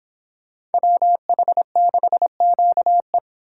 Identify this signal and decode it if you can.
Morse code, heard as a keyed tone, about 31 wpm, 700 Hz. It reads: WH6QE